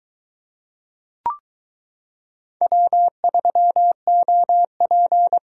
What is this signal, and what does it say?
Morse code, heard as a keyed tone, about 22 wpm, 700 Hz. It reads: W3OP